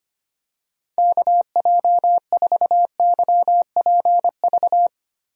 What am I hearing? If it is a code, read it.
Morse code, heard as a keyed tone, about 25 wpm, 700 Hz. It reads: KJ4YPV
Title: USA Call Signs - 6 Characters - Repeated 3x - 25wpm